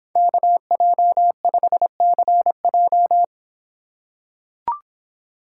Morse code, heard as a keyed tone, about 24 wpm, 700 Hz. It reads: KJ5CJ